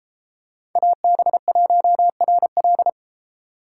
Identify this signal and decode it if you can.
Morse code, heard as a keyed tone, about 33 wpm, 700 Hz. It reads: AB1RL